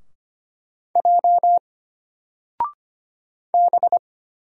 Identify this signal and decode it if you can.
Morse code, heard as a keyed tone, about 25 wpm, 700 Hz. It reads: J B
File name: Single-Letter-Number-25wpm